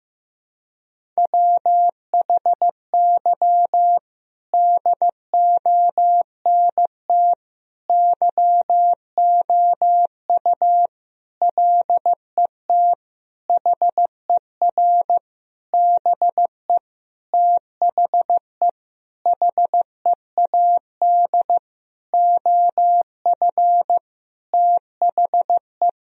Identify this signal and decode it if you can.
Morse code, heard as a keyed tone, about 15 wpm, 700 Hz. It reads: WHY DONT YOU LET HER BE THE HEAD OF THE
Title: Sentences from Top 1000 Words - Review Part 1 - 15wpm